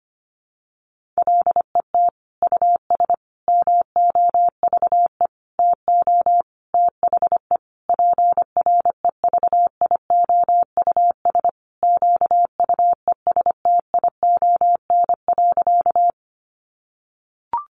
Morse code, heard as a keyed tone, about 25 wpm, 700 Hz. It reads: LET US MOVE TO THE PREVIOUS QUESTION.
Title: Sentences from Top 1500 Words 25wpm